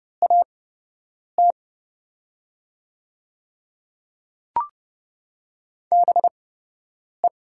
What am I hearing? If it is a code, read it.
Morse code, heard as a keyed tone, about 30 wpm, spaced as for 8 wpm, 700 Hz. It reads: AT BE
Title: 8x Character Spacing - Top 100 Words - 30wpm